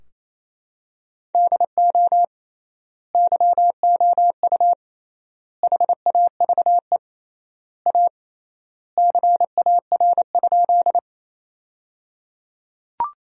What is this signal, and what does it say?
Morse code, heard as a keyed tone, about 28 wpm, 700 Hz. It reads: DO YOU HAVE A CAR?
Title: 3x Word-Spacing - Sentences from Top 300 Words - 28wpm